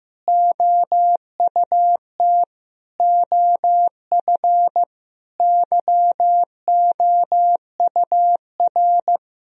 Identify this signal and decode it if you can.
Morse code, heard as a keyed tone, about 16 wpm, 700 Hz. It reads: OUT OF YOUR
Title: Sentences from Top 1000 Words - Review Part 2 - 15wpm